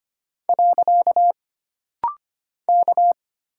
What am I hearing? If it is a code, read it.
Morse code, heard as a keyed tone, about 25 wpm, 700 Hz. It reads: . K
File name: Single Character (Letter:Number:Punctuation:Prosign) - Rapid-Fire - 25wpm